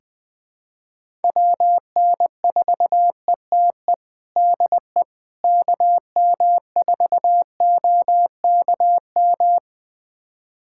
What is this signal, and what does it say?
Morse code, heard as a keyed tone, about 20 wpm, 700 Hz. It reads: WN4ETE DE KM4OKM
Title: USA Callsign DE Callsign 20wpm